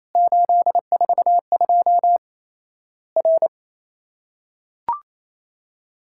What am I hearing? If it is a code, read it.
Morse code, heard as a keyed tone, about 28 wpm, 700 Hz. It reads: 842 R
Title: Worked All Europe DX Contest - QTC Exchange - 28wpm